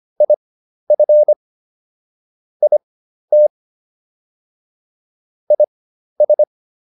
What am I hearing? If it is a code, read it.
Morse code, heard as a keyed tone, about 25 wpm, spaced as for 12 wpm, 600 Hz. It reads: IF IT IS